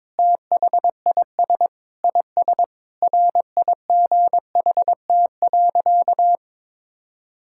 Morse code, heard as a keyed tone, about 22 wpm, 700 Hz. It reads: THIS IS RIGHT.
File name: practice / 22wpm / Sentences with 3-Word Phrases - 22wpm